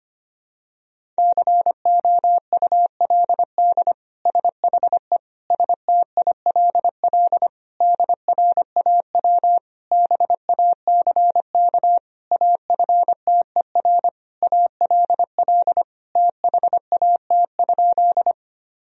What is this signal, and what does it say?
Morse code, heard as a keyed tone, about 25 wpm, 700 Hz. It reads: COULD SHE STILL DRAW BACK AFTER ALL THAT?
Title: Sentences from Top 1000 Words - Review Part 1 - 25wpm